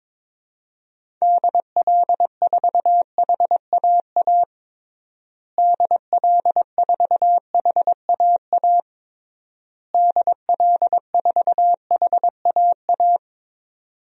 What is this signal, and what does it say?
Morse code, heard as a keyed tone, about 22 wpm, 700 Hz. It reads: DL4HAA DL4HAA DL4HAA
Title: DXCC-100 Call Signs - 6 Characters - Repeated 3x - 22wpm